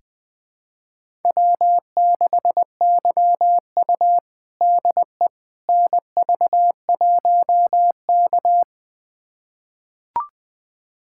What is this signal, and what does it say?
Morse code, heard as a keyed tone, about 20 wpm, 700 Hz. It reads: W6YU DE NV1K
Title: USA Callsign DE Callsign - 4 Characters - 20wpm